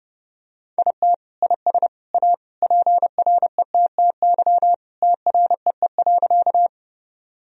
Morse code, heard as a keyed tone, about 30 wpm, 700 Hz. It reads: IT IS A PRETTY TREE.
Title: Sentences from Top 1000 Words - Review Part 1 - 30wpm